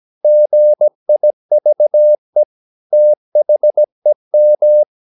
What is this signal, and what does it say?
Morse code, heard as a keyed tone, about 17 wpm, 600 Hz. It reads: GIVE THEM
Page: Romanen Never af Ken Follet på engelsk.